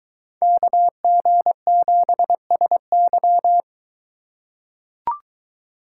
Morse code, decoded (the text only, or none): KG7SY